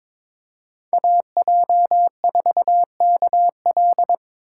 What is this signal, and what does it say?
Morse code, heard as a keyed tone, about 22 wpm, 700 Hz. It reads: AJ4KL
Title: USA Call Signs - 5 Characters - 4x Character Spacing - 22wpm